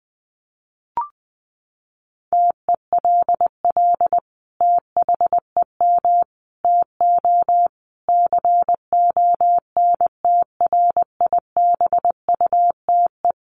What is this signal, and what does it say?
Morse code, heard as a keyed tone, about 20 wpm, 700 Hz. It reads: TELL THEM TO CONTRIBUTE
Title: Sentences from Top 1400 Words 20wpm